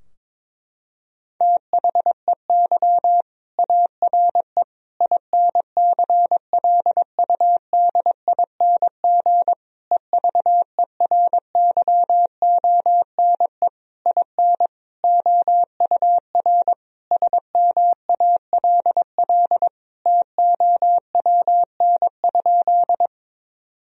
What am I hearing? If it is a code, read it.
Morse code, heard as a keyed tone, about 22 wpm, 700 Hz. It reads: THEY ARE INCLUDING EVERYONE IN OUR SMALL TOWN?
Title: Sentences from Top 1000 Words - Review Part 4 - 22wpm